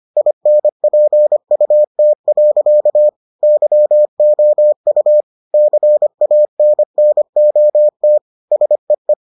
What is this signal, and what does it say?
Morse code, heard as a keyed tone, about 25 wpm, 600 Hz. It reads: INPUT. YOU CANNOT SEE